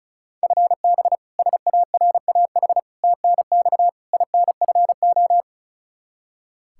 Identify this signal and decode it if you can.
Morse code, heard as a keyed tone, about 36 wpm, 700 Hz. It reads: FB SARAH TNX INFO
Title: K1USN SST QSO Practice 35wpm